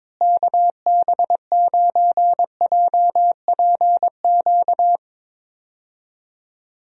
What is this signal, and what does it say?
Morse code, heard as a keyed tone, about 22 wpm, 700 Hz. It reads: KB9JPQ
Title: USA Call Signs - 6 Characters - Repeated 3x - 22wpm